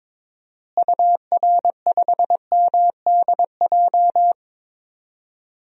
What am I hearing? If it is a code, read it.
Morse code, heard as a keyed tone, about 21 wpm, 700 Hz. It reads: UR5MDJ
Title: DXCC-100 Call Signs - 6 Characters - Repeated 3x - 22wpm